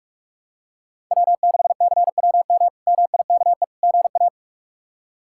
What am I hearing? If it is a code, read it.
Morse code, heard as a keyed tone, about 43 wpm, 700 Hz. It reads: W6CWM MIKE GA
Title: K1USN SST QSO Practice - 45wpm